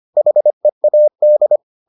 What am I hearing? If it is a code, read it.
Morse code, heard as a keyed tone, about 25 wpm, 600 Hz. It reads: HEAD